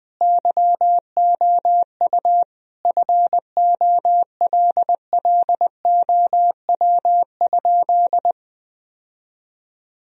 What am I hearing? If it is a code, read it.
Morse code, heard as a keyed tone, about 20 wpm, 700 Hz. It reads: YOU FOLLOW?
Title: Sentences from Top 1000 Words - Review Part 3 - 20wpm